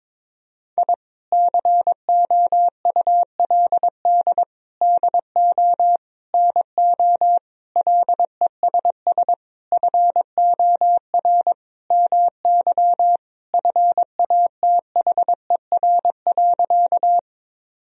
Morse code, heard as a keyed tone, about 22 wpm, 700 Hz. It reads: I COULD DO NO LESS FOR MY FATHER.
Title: Sentences from Top 1000 Words - Review Part 2 - 22wpm